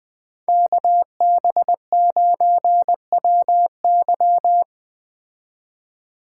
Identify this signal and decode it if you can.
Morse code, heard as a keyed tone, about 20 wpm, 700 Hz. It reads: KB9WY
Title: Speed Racing - USA 2x2 Call Signs 17wpm